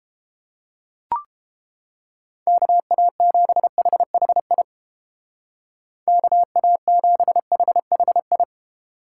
Morse code, heard as a keyed tone, about 31 wpm, 700 Hz. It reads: KA7HHI KA7HHI